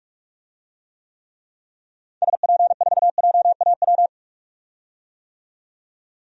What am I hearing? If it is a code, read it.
Morse code, heard as a keyed tone, about 41 wpm, 700 Hz. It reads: SP4JAW
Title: DXCC-100 Call Signs - 6 Characters - 45wpm